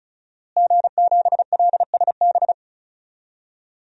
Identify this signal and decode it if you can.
Morse code, heard as a keyed tone, about 36 wpm, 700 Hz. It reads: G7LSB
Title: DXCC-100 Call Signs - 5 Characters - Repeated 2x - 35wpm